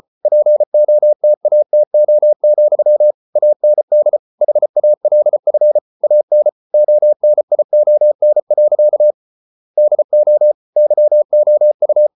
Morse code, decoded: POTATO, AND HALF AN ONION. DO YOU

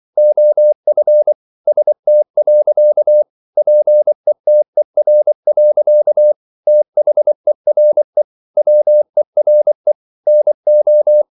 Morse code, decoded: OF ST. PETER. THERE WERE NO

